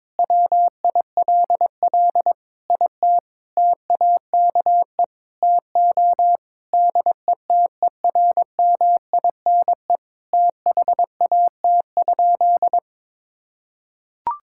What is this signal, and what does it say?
Morse code, heard as a keyed tone, about 22 wpm, 700 Hz. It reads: WILL IT TAKE TO DETERMINE THAT?
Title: Sentences from Top 1000 Words - Review Part 5 - 22wpm